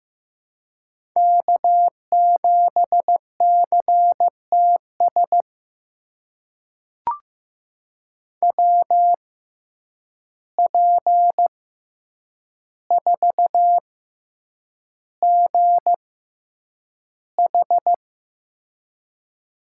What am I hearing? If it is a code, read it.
Morse code, heard as a keyed tone, about 15 wpm, spaced as for 8 wpm, 700 Hz. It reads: K7CTS WP4GH
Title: USA Call Signs - 5 Characters - 6x Character Spacing - 15wpm